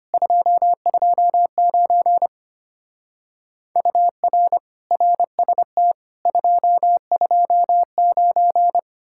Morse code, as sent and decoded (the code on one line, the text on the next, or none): ..--- ..--- ----. / ..- .-. / .-. ... - / ..--- ..--- ----.
229 UR RST 229